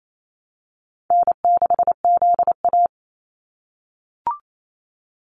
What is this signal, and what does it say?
Morse code, heard as a keyed tone, about 28 wpm, 700 Hz. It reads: N6ZA
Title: NA SOTA Activator Callsigns 28wpm